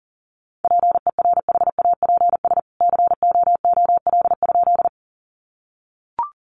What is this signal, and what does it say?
Morse code, heard as a keyed tone, about 40 wpm, 700 Hz. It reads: PERHAPS COOL?
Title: Sentences from Top 1200 Words 40wpm